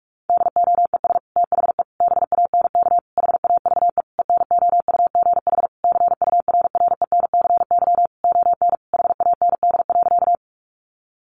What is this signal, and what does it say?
Morse code, heard as a keyed tone, about 44 wpm, 700 Hz. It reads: DOES THE BANK HAVE ENOUGH CURRENCY ON HAND.
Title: Sentences from Top 1500 Words 45wpm